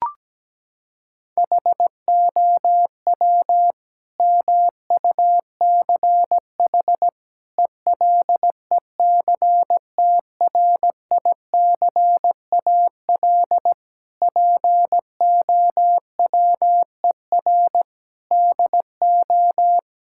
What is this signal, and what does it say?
Morse code, heard as a keyed tone, about 17 wpm, 700 Hz. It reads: HOW MUCH ELECTRICAL POWER DO
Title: Sentences from Top 1000 Words - Review Part 6 - 17wpm